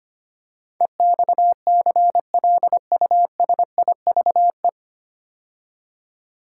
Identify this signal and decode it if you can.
Morse code, heard as a keyed tone, about 25 wpm, 700 Hz. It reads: EXCLUSIVE
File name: practice / 25wpm / 9 Letter Words - First 3 Letters Given - 25wpm